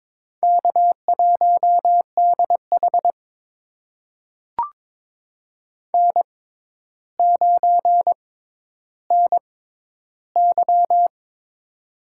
Morse code, heard as a keyed tone, about 22 wpm, spaced as for 11 wpm, 700 Hz. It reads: K1DH N9NY
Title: USA Call Signs - 4 Characters - 6x Character Spacing - 22wpm